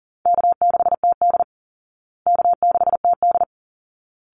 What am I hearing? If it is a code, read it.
Morse code, heard as a keyed tone, about 40 wpm, 700 Hz. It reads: K6TD K6TD
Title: Easy Call Signs Repeated 2x - 40wpm